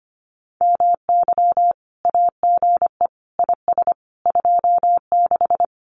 Morse code, heard as a keyed tone, about 25 wpm, 700 Hz. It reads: MY AGE IS 26
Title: Common QSO Phrases Encore 25wpm